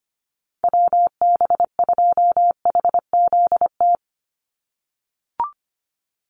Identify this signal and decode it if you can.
Morse code, heard as a keyed tone, about 25 wpm, 700 Hz. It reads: WB2HZT
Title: USA Call Signs 25wpm